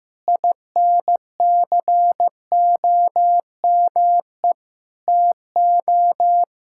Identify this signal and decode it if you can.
Morse code, heard as a keyed tone, about 15 wpm, 700 Hz. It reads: INCOME TO
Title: Sentences from Top 1000 Words - Review Part 5 - 15wpm